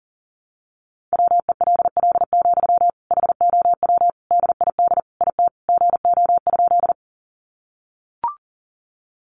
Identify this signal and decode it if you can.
Morse code, heard as a keyed tone, about 40 wpm, 700 Hz. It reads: WELL, HOW DID IT GO?
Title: Sentences from Top 200 Words 40wpm